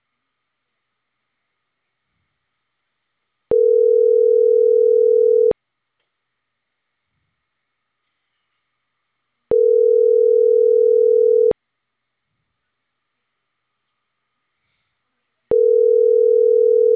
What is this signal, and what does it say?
Ringback tone (North America), 440 + 480 Hz.